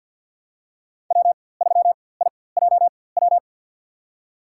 Morse code, heard as a keyed tone, about 47 wpm, spaced as for 23 wpm, 700 Hz. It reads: W3IJW
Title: USA Call Signs - 5 Characters - 4x Character Spacing - 50wpm